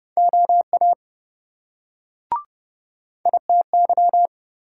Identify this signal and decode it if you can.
Morse code, heard as a keyed tone, about 26 wpm, 700 Hz. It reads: OA ITY